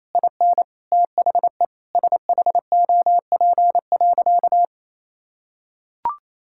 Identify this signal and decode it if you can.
Morse code, heard as a keyed tone, about 28 wpm, 700 Hz. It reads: IN THE SHOP.
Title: Sentences from Top 1600 Words 28wpm